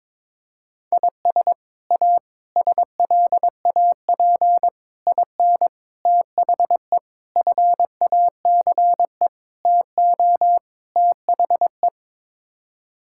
Morse code, heard as a keyed tone, about 22 wpm, 700 Hz. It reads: IS A SLAP IN THE FACE TO THE
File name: practice / 22wpm / Sets of 8 Words - 22wpm